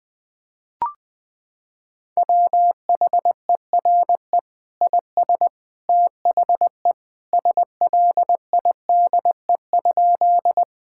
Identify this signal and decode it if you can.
Morse code, heard as a keyed tone, about 20 wpm, 700 Hz. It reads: WHERE IS THE SLIDE?
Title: Sentences from Top 1600 Words 20wpm